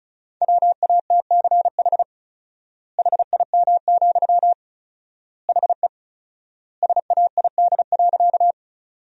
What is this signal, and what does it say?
Morse code, heard as a keyed tone, about 36 wpm, 700 Hz. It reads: WATCH HIM, HE SAID.
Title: 4x Word-Spacing - Sentences from Top 300 Words - 35wpm